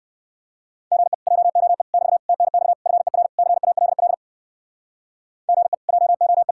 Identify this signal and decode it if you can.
Morse code, heard as a keyed tone, about 58 wpm, 700 Hz. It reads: GE JOE <BT> TNX FER CALL GE JOE